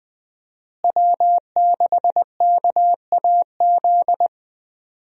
Morse code, heard as a keyed tone, about 20 wpm, 700 Hz. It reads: W6KAZ
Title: USA Call Signs - 5 Characters - Repeated 3x - 20wpm